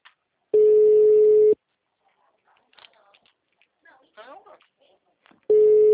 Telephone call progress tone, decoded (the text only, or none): ringback tone